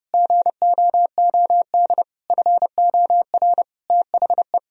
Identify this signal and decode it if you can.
Morse code, heard as a keyed tone, about 30 wpm, 700 Hz. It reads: GOOD FOR THE
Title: Sentences from Top 1000 Words - Review Part 4 - 30wpm